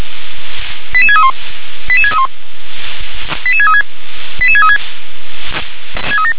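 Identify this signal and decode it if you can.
Five-tone selective call (ZVEI), heard as quick runs of five tones, tones of about 70 ms. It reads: 88631 88631 88636 88636
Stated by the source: ZVEI five-tone selective call received on an SDR.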